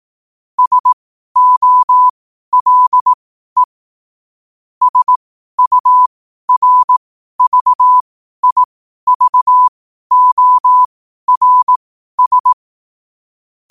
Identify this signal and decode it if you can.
Morse code, heard as a keyed tone, about 18 wpm, spaced as for 12 wpm, 1000 Hz. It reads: SOLE SURVIVORS
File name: qotdfiles / 12wpm